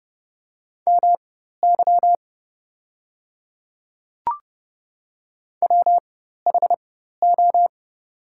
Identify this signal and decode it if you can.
Morse code, heard as a keyed tone, about 30 wpm, spaced as for 14 wpm, 700 Hz. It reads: MY WHO
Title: 4x Character Spacing - Top 100 Words - 30wpm